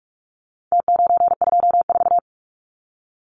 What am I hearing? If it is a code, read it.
Morse code, heard as a keyed tone, about 44 wpm, 700 Hz. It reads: T924
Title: Numbers - Contest Style - 4-Digit 45wpm